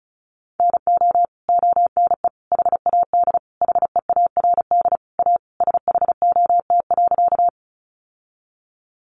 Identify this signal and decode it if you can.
Morse code, heard as a keyed tone, about 34 wpm, 700 Hz. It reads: NO ONE HAD HEARD A SHOT.
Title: Sentences from Top 1100 Words 35wpm